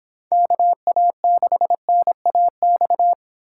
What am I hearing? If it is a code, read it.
Morse code, heard as a keyed tone, about 26 wpm, 700 Hz. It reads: KA6NAX
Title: Speed Racing - USA 2x3 Call Signs 22wpm